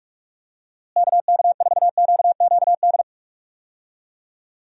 Morse code, heard as a keyed tone, about 46 wpm, 700 Hz. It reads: KK4QQD
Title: USA Call Signs - 6 Characters - Repeated 2x - 45wpm